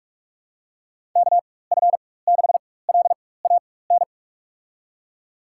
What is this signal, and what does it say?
Morse code, heard as a keyed tone, about 42 wpm, spaced as for 20 wpm, 700 Hz. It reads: KF6LAN
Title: USA Call Signs - 6 Characters - 4x Character Spacing - 45wpm